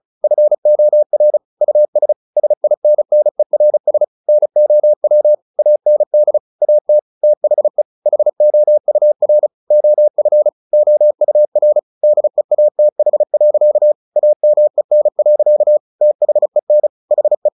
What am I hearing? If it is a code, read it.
Morse code, heard as a keyed tone, about 34 wpm, 600 Hz. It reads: FOR US SINNERS NOW AND AT THE HOUR OF OUR DEATH. AMEN. THEN HE
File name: the old man and the sea / the old man and the sea 1 35wpm